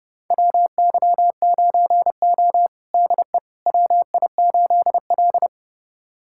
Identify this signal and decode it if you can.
Morse code, heard as a keyed tone, about 30 wpm, 700 Hz. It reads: WY9O DE WI8L
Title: USA Callsign DE Callsign - 4 Characters - 30wpm